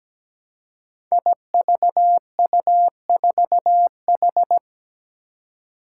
Morse code, heard as a keyed tone, about 17 wpm, 700 Hz. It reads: IVU4H
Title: Five Challenging Letter-Number Combinations 17wpm